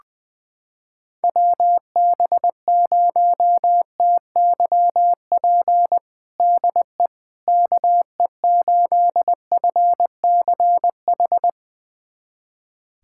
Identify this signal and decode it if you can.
Morse code, heard as a keyed tone, about 21 wpm, 700 Hz. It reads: WB0TYP DE KE8FCH
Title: USA Callsign DE Callsign - 6 Characters - 20wpm